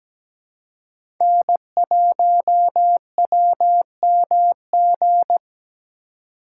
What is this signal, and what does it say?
Morse code, heard as a keyed tone, about 17 wpm, 700 Hz. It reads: N1WMG
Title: USA Call Signs - 5 Characters - 4x Character Spacing - 17wpm